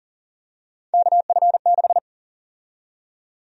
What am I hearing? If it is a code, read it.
Morse code, heard as a keyed tone, about 40 wpm, 700 Hz. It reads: KF6
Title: USA Call Sign Prefixes - 40wpm